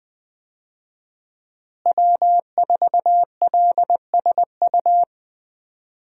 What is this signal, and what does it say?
Morse code, heard as a keyed tone, about 20 wpm, 700 Hz. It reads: W4LSU
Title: USA Call Signs - 5 Characters - Repeated 3x - 20wpm